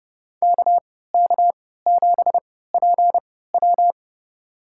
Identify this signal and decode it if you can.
Morse code, heard as a keyed tone, about 30 wpm, spaced as for 17 wpm, 700 Hz. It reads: KK7PW
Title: USA Call Signs - 5 Characters - 3x Character Spacing - 30wpm